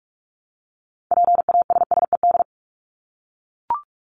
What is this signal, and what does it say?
Morse code, heard as a keyed tone, about 44 wpm, 700 Hz. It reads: PASSED
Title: ED Suffix - Top 100 Words with Base Words 45wpm